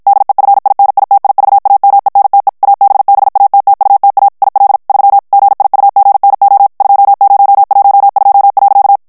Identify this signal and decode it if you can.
Morse code, heard as a keyed tone, about 56 wpm, 800 Hz. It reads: DEFINITIVAMENTE ABBATTUTA IL 3 GIUGNO 2012.
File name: morse mp3 / ITALIEN texte clair 1 vitesse 50 mots minute